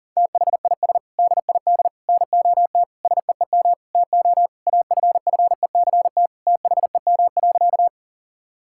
Morse code, heard as a keyed tone, about 40 wpm, 700 Hz. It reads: THIS DID NOT SEEM TO AFFECT THEM.
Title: Sentences from Top 1000 Words - Review Part 6 - 40wpm